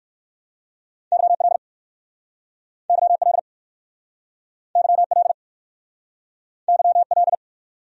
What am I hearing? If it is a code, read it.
Morse code, heard as a keyed tone, about 55 wpm, 700 Hz. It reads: YL YL YL YL
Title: Speed-Racing Abbreviations - 45wpm